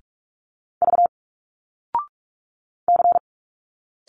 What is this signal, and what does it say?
Morse code, heard as a keyed tone, about 46 wpm, 700 Hz. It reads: V C
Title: Single Letters 45wpm